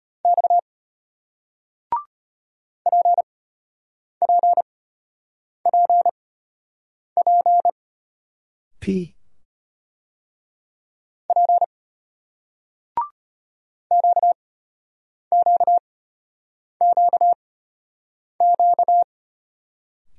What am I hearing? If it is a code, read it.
Morse code, heard as a keyed tone, about 35 wpm, spaced as for 8 wpm, 700 Hz. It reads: X PPPP P QQQQ